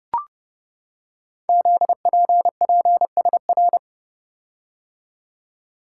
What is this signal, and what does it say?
Morse code, heard as a keyed tone, about 28 wpm, 690 Hz. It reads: ZPPSR
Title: Code Groups - 5 Letters - 30wpm